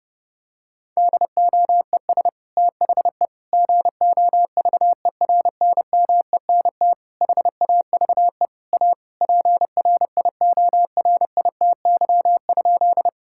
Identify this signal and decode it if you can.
Morse code, heard as a keyed tone, about 28 wpm, 700 Hz. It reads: DOES THE GOVERNMENT HAVE A PRIORITY?
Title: Sentences from Top 1600 Words 30wpm